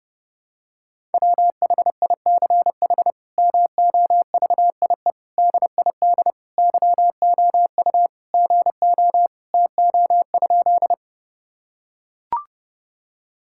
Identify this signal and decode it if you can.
Morse code, heard as a keyed tone, about 30 wpm, 700 Hz. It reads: WHICH MOVIE DID YOU GO TO?
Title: Sentences from Top 1000 Words - Review Part 5 - 30wpm